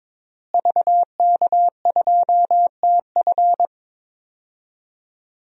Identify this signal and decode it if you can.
Morse code, heard as a keyed tone, about 22 wpm, 700 Hz. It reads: VK2TF
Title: DXCC-100 Call Signs - 5 Characters - Repeated 3x - 22wpm